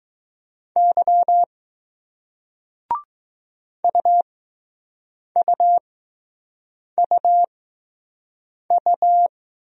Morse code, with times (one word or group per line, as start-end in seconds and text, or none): 0.8-1.4 Y
3.8-9.3 UUUU